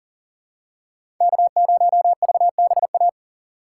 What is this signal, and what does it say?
Morse code, heard as a keyed tone, about 40 wpm, 700 Hz. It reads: K0VBA
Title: USA Call Signs - 5 Characters - 6x Character Spacing - 40wpm